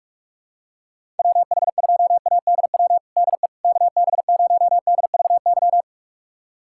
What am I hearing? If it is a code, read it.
Morse code, heard as a keyed tone, about 47 wpm, 700 Hz. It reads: WH2ADW DE KB0DVY
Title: USA Callsign DE Callsign - 6 Characters - 45wpm